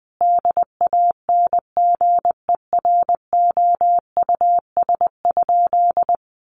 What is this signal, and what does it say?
Morse code, heard as a keyed tone, about 20 wpm, 700 Hz. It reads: DANGEROUS?
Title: Sentences from Top 1200 Words 20wpm